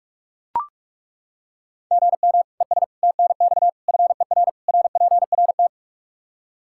Morse code, heard as a keyed tone, about 41 wpm, 700 Hz. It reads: GM ES TNX FER RPRT
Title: Practice QSO - Exchange 1 - Beginning - 45wpm